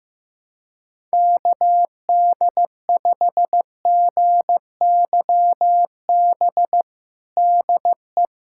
Morse code, heard as a keyed tone, about 15 wpm, 700 Hz. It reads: KD5GYB DE
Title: USA Callsign DE Callsign - 6 Characters - 15wpm